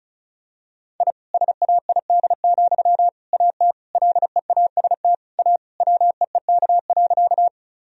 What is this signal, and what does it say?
Morse code, heard as a keyed tone, about 36 wpm, 700 Hz. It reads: I SAID, AT LEAST A WEEK.
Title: Sentences from Top 1000 Words - Review Part 2 - 35wpm